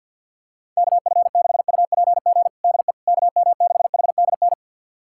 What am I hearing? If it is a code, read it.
Morse code, heard as a keyed tone, about 50 wpm, 700 Hz. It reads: KF6UPG DE KM6HDN
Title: USA Callsign DE Callsign - 6 Characters - 50wpm